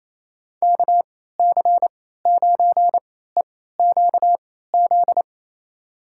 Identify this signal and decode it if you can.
Morse code, heard as a keyed tone, about 28 wpm, spaced as for 16 wpm, 700 Hz. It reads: KC9EQZ
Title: USA Call Signs - 6 Characters - 3x Character Spacing - 28wpm